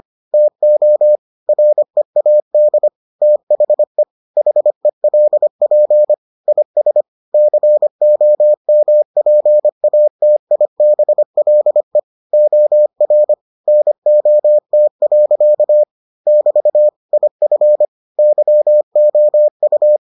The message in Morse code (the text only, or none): TO READ THE HELP IS COMPATIBLE OR NOT. <BT> IF YOU